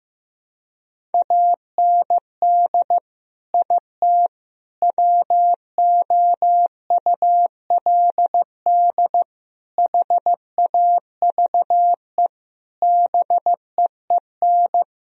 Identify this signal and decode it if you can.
Morse code, heard as a keyed tone, about 16 wpm, 700 Hz. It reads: AND IT WOULD HAVE BEEN
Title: Sentences from Top 1000 Words - Review Part 4 - 15wpm